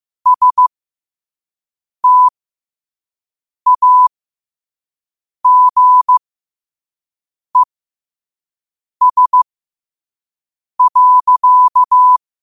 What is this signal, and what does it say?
Morse code, heard as a keyed tone, about 15 wpm, spaced as for 5 wpm, 1000 Hz. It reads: STAGES.
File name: qotdfiles / 5wpm